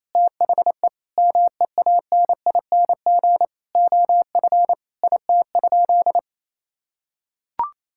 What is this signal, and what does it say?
Morse code, heard as a keyed tone, about 27 wpm, 700 Hz. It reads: THE MEANING OF IT?
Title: Sentences from Top 1000 Words - Review Part 6 - 28wpm